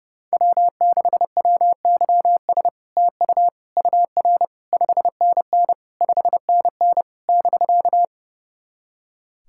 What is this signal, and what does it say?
Morse code, heard as a keyed tone, about 30 wpm, 700 Hz. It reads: W6WYS TU UR 5NN 5NN <BK>
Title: SOTA QSO Practice 30wpm